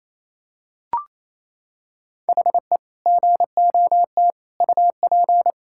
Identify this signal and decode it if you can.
Morse code, heard as a keyed tone, about 28 wpm, 700 Hz. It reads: HE GOT UP
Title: Sentences from Top 1600 Words 28wpm